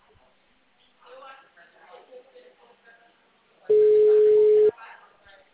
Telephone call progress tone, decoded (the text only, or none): ringback tone